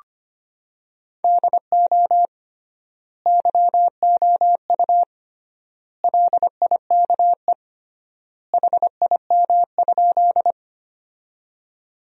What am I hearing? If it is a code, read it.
Morse code, heard as a keyed tone, about 25 wpm, 700 Hz. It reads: DO YOU LIKE HIM?
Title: 3x Word-Spacing - Sentences from Top 100 Words - 25wpm